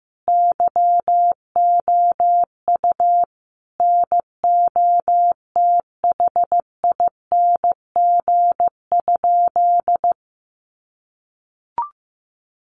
Morse code, heard as a keyed tone, about 14 wpm, 700 Hz. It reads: YOU NOTHING?
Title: Sentences from Top 500 Words 15wpm